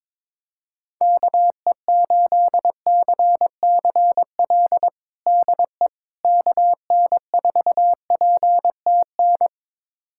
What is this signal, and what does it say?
Morse code, heard as a keyed tone, about 22 wpm, 700 Hz. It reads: KE8CCL DE KN4PTN
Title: USA Callsign DE Callsign - 6 Characters - 22wpm